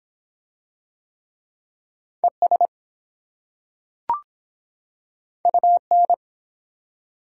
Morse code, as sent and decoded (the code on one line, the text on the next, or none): . ... / ..- -.
ES UN